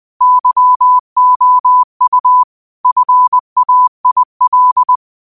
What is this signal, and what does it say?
Morse code, heard as a keyed tone, about 20 wpm, 1000 Hz. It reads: YOU FAIL
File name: qotdfiles / 20wpm